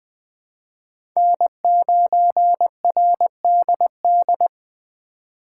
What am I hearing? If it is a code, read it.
Morse code, heard as a keyed tone, about 20 wpm, 700 Hz. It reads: N9RDD
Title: USA Call Signs - 5 Characters - 3x Character Spacing - 20wpm